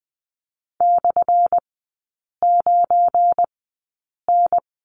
Morse code, heard as a keyed tone, about 20 wpm, spaced as for 8 wpm, 700 Hz. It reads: /9N
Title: 218 Lesson 35e - Number 0 - Three Characters 20wpm